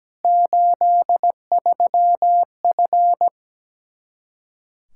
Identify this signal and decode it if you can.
Morse code, heard as a keyed tone, about 17 wpm, 700 Hz. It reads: 83F
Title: Practice QSO - Exchange 2 - Info - 17wpm